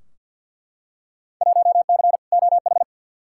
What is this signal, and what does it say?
Morse code, heard as a keyed tone, about 51 wpm, 700 Hz. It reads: 1C OH
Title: Field Day Exchanges - 50wpm